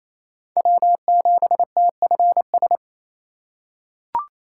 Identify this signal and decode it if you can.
Morse code, heard as a keyed tone, about 28 wpm, 700 Hz. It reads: W7TFS
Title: USA Call Signs - 5 Characters - 2x Character Spacing - 28wpm